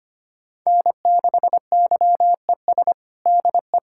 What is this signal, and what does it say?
Morse code, heard as a keyed tone, about 25 wpm, 700 Hz. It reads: N6YES DE
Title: K1USN SST QSO Practice 25wpm